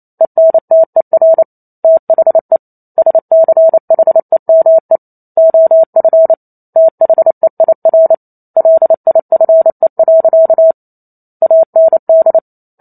Morse code, heard as a keyed tone, about 28 wpm, 650 Hz. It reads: ENTER THE SCHEME OF THEIR LIFE. AND